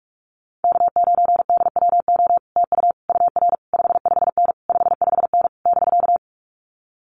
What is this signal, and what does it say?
Morse code, heard as a keyed tone, about 46 wpm, 700 Hz. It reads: K9DWO TU UR 55N 55N <BK>
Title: Common QSO Phrases Encore 45wpm